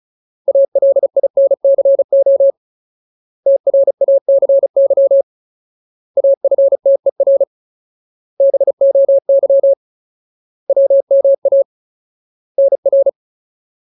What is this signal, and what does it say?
Morse code, heard as a keyed tone, about 34 wpm, 550 Hz. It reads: ALINCO TRACY AFTER BOY WMA NR
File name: Ultimate Word List / 35WPM Ultimate